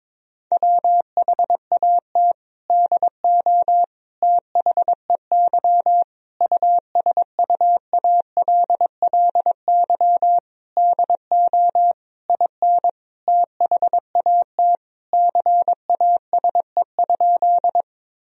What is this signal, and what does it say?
Morse code, heard as a keyed tone, about 22 wpm, 700 Hz. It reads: WHAT DO THEY USUALLY DO IN THAT CASE?
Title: Sentences from Top 1000 Words - Review Part 3 - 22wpm